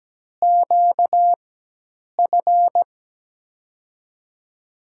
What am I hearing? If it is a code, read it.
Morse code, heard as a keyed tone, about 17 wpm, spaced as for 8 wpm, 700 Hz. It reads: QF
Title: USA Call Signs - 6 Characters - 4x Character Spacing - 17wpm